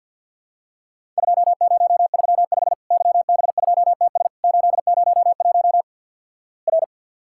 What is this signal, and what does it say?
Morse code, heard as a keyed tone, about 50 wpm, 700 Hz. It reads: 2035 YB2TS 801 R
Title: Worked All Europe DX Contest - QTC Exchange with Repeats - 50wpm